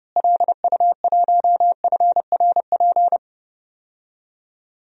Morse code, heard as a keyed tone, about 30 wpm, 700 Hz. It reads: LU1FRP
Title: DXCC-100 Call Signs - 6 Characters - Repeated 2x - 30wpm